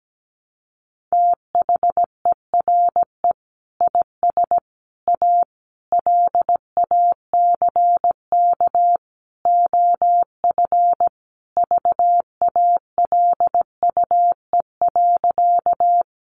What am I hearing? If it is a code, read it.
Morse code, heard as a keyed tone, about 17 wpm, 700 Hz. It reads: THERE IS A LACK OF VALUE.
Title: Sentences from Top 1100 Words 17wpm